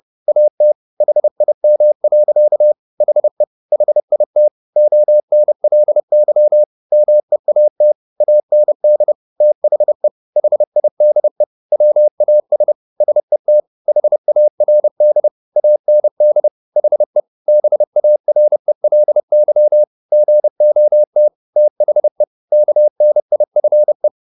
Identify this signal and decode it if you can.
Morse code, heard as a keyed tone, about 30 wpm, 600 Hz. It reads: AT HIM. HE HIT ONLY MEAT AND THE HIDE WAS SET HARD AND HE BARELY GOT THE KNIFE